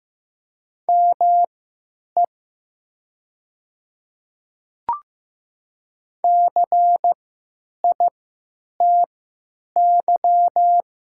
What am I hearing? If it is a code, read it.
Morse code, heard as a keyed tone, about 15 wpm, spaced as for 9 wpm, 700 Hz. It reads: ME CITY
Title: 3x Character Spacing - Top 300 Words - 15wpm